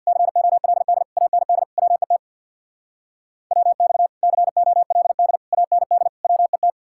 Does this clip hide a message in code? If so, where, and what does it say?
Yes: Morse code from 0.1 to 6.7 s, heard as a keyed tone, about 65 wpm, 700 Hz. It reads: COLD AND WET WX COLD AND WET